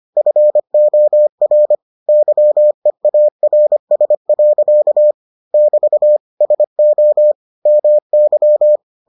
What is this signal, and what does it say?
Morse code, heard as a keyed tone, about 25 wpm, 600 Hz. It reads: FOR YEARS. <BT> SO MY